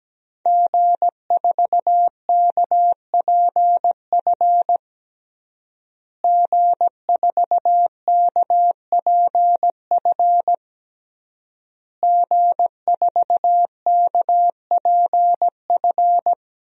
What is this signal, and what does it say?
Morse code, heard as a keyed tone, about 17 wpm, 700 Hz. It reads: G4KPF G4KPF G4KPF
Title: DXCC-100 Call Signs - 5 Characters - Repeated 3x - 17wpm